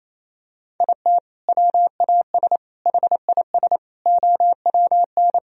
Morse code, heard as a keyed tone, about 28 wpm, 700 Hz. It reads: IT WAS HIS OWN
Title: Sentences from Top 1000 Words - Review Part 3 - 28wpm